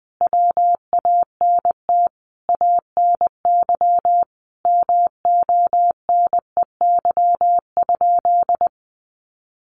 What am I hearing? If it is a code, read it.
Morse code, heard as a keyed tone, about 20 wpm, 700 Hz. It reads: WANT ANY MONEY?
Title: Sentences from Top 400 Words 20wpm